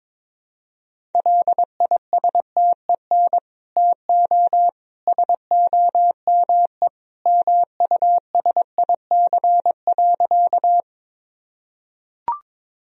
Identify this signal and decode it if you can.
Morse code, heard as a keyed tone, about 22 wpm, 700 Hz. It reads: LISTEN TO SOME MUSIC.
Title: Sentences from Top 1000 Words - Review Part 1 - 22wpm